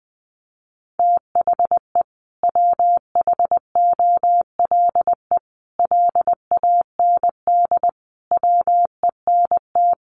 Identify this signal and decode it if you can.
Morse code, heard as a keyed tone, about 20 wpm, 700 Hz. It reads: THE WHOLE LAND WENT
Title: Sentences from Top 1400 Words 20wpm